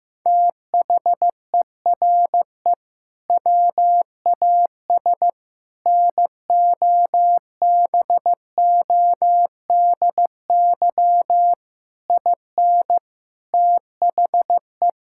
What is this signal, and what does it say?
Morse code, heard as a keyed tone, about 15 wpm, 700 Hz. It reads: THERE WAS NOBODY IN THE
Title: Sentences from Top 1600 Words 15wpm